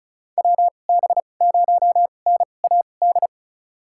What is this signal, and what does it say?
Morse code, heard as a keyed tone, about 36 wpm, spaced as for 25 wpm, 700 Hz. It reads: WB0NAD
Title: USA Call Signs - 6 Characters - 2x Character Spacing - 35wpm